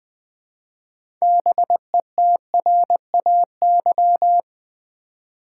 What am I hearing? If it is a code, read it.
Morse code, heard as a keyed tone, about 20 wpm, 700 Hz. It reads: BETRAY
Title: 6 Letter Words - First 3 Letters Given - 20wpm